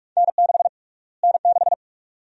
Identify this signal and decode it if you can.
Morse code, heard as a keyed tone, about 46 wpm, 700 Hz. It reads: N6 N6
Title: USA Call Sign Prefixes - Repeated 2x - 45wpm